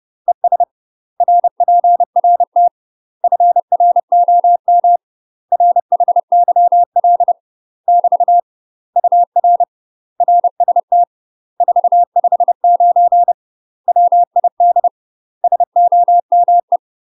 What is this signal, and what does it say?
Morse code, heard as a keyed tone, about 30 wpm, 700 Hz. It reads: ES RPRT FROM RHYL <BT> UR RST 459 WID SOME
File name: UK MASTER QSOs TRAINING / 30WPM / UK MASTER QSOs 0047